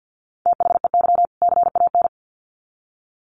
Morse code, heard as a keyed tone, about 50 wpm, 700 Hz. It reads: THEY CAN
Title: Sets of 2 Words 50wpm